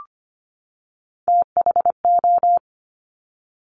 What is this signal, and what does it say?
Morse code, heard as a keyed tone, about 25 wpm, 700 Hz. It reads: THO
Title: Top Three Letter Combinations in Words 25wpm